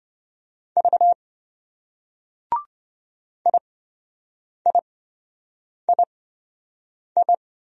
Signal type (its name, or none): Morse code